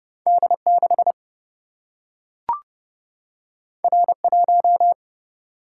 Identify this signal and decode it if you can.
Morse code, heard as a keyed tone, about 29 wpm, 700 Hz. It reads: D6 R1
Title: Single-Letter Single-Number 30wpm